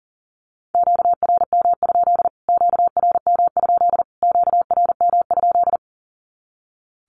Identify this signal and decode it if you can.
Morse code, heard as a keyed tone, about 40 wpm, 700 Hz. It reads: QRM? QRM? QRM?
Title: Q-Codes Repeated 3x 40wpm